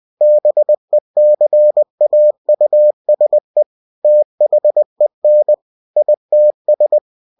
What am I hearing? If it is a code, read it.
Morse code, heard as a keyed tone, about 20 wpm, 600 Hz. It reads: BECAUSE THEN ITS
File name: cw-books / winnie-the-pooh / winnie-20wpm-page-0011